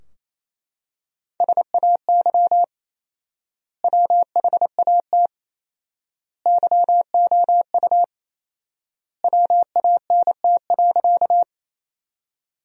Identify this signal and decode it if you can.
Morse code, heard as a keyed tone, about 28 wpm, 700 Hz. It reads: SAY WHAT YOU WANT.
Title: 4x Word-Spacing - Sentences from Top 200 Words - 28wpm